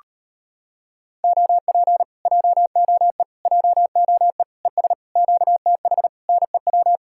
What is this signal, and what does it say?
Morse code, heard as a keyed tone, about 33 wpm, 700 Hz. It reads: OP JOE JOE ES QTH NEW